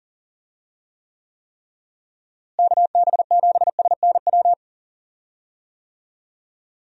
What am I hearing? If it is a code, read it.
Morse code, heard as a keyed tone, about 39 wpm, 700 Hz. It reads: KB7SNW
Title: USA Call Signs - 6 Characters - Repeated 3x - 40wpm